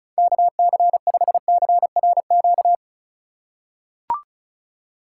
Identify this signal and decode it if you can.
Morse code, heard as a keyed tone, about 36 wpm, 700 Hz. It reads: KC5CRQ
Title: USA Call Signs - 6 Characters - 6x Character Spacing - 35wpm